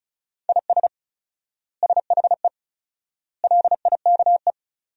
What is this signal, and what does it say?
Morse code, heard as a keyed tone, about 36 wpm, 700 Hz. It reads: IS SHE LIKE
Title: 4x Word-Spacing - Sentences from Top 100 Words - 35wpm